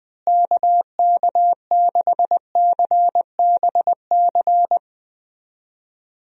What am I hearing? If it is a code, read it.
Morse code, heard as a keyed tone, about 20 wpm, 700 Hz. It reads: KK6CBC
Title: USA Call Signs - 6 Characters - Repeated 3x - 20wpm